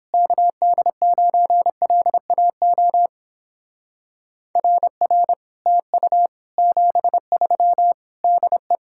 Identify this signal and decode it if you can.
Morse code, heard as a keyed tone, about 29 wpm, 700 Hz. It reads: KD9LAO RR TU 73 DE